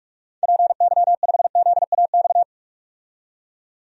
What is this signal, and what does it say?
Morse code, heard as a keyed tone, about 46 wpm, 700 Hz. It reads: PY5ZAX
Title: DXCC-100 Call Signs - 6 Characters - 45wpm